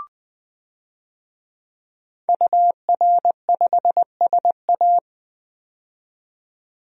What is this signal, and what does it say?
Morse code, heard as a keyed tone, about 20 wpm, 700 Hz. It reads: UR5SA
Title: DXCC-100 Call Signs - 5 Characters - 20wpm